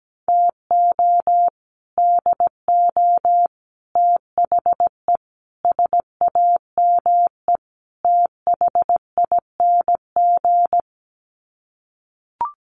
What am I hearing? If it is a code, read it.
Morse code, heard as a keyed tone, about 17 wpm, 700 Hz. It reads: TO DO THE SAME THING
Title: Sets of 5 Words Encore 17wpm